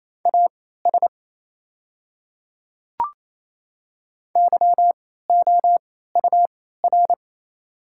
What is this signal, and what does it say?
Morse code, heard as a keyed tone, about 27 wpm, spaced as for 16 wpm, 700 Hz. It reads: AS YOUR
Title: 3x Character Spacing - Top 100 Words - 28wpm